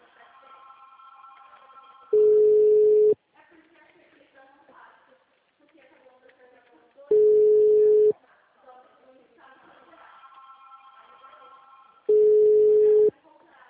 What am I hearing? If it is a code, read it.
Ringback tone (Europe, 425 Hz), with speech.